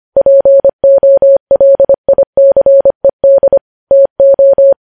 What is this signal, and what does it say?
Morse code, heard as a keyed tone, about 25 wpm, 550 Hz. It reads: POLICED TO
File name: podcast / News headlines-1x-25x25